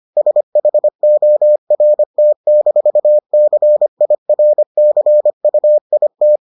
Morse code, heard as a keyed tone, about 25 wpm, 600 Hz. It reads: SHORT-CIRCUIT